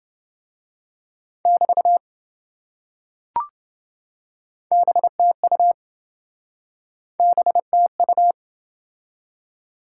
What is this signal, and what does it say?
Morse code, heard as a keyed tone, about 27 wpm, 700 Hz. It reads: <BT> BTU BTU